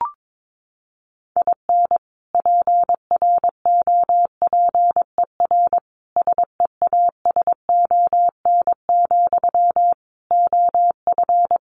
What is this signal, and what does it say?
Morse code, heard as a keyed tone, about 22 wpm, 700 Hz. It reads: IN PROPER SEASON, OF
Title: Sentences from Top 1400 Words 22wpm